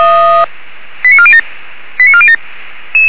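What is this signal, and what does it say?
Five-tone selective call (ZVEI), heard as quick runs of five tones, tones of about 70 ms. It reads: 89407 89407
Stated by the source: ZVEI five-tone selective call 89406 received via SDR.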